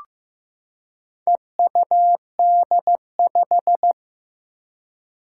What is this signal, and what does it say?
Morse code, heard as a keyed tone, about 15 wpm, 700 Hz. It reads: EUD5
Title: Four Challenging Letter-Number Combinations 15wpm